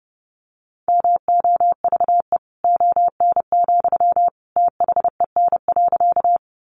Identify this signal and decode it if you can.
Morse code, heard as a keyed tone, about 28 wpm, 700 Hz. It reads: MOVE ON, THEN.
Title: Sentences from Top 200 Words 30wpm